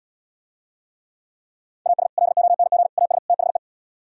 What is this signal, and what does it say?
Morse code, heard as a keyed tone, about 77 wpm, 700 Hz. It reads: AS LONG AS THE